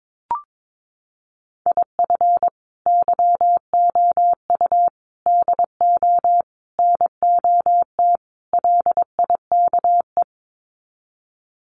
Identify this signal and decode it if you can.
Morse code, heard as a keyed tone, about 22 wpm, 700 Hz. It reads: IF YOU DO NOT LIKE
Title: Sets of 5 Words 22wpm